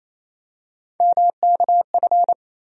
Morse code, heard as a keyed tone, about 28 wpm, 700 Hz. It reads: MKF
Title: USA Call Sign Postfixes - 3 Letters 28wpm